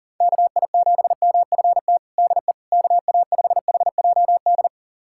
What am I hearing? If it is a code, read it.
Morse code, heard as a keyed tone, about 40 wpm, 700 Hz. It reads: KI7MFT DE KA5HJD
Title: USA Callsign DE Callsign - 6 Characters - 40wpm